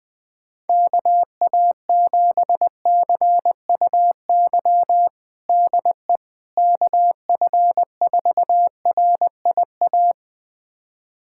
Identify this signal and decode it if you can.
Morse code, heard as a keyed tone, about 20 wpm, 700 Hz. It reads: KA7CUY DE KF4RIA
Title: USA Callsign DE Callsign 20wpm